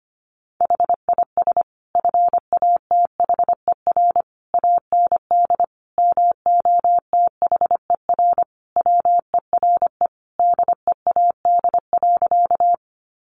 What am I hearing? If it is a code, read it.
Morse code, heard as a keyed tone, about 25 wpm, 700 Hz. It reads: HIS FATHER AND MOTHER WERE DEAD.
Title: Sentences from Top 1500 Words 25wpm